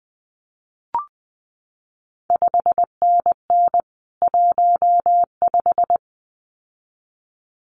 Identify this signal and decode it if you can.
Morse code, heard as a keyed tone, about 20 wpm, 700 Hz. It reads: 5NN 15
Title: CQ WW DX Contest Exchange - 20wpm